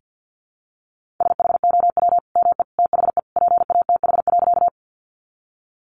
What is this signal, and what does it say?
Morse code, heard as a keyed tone, about 50 wpm, 700 Hz. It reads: SHOW ME THE PATH.
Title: Sentences from Top 1400 Words 50wpm